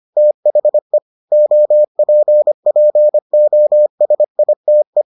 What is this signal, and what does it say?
Morse code, heard as a keyed tone, about 25 wpm, 600 Hz. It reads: THE OPPOSITE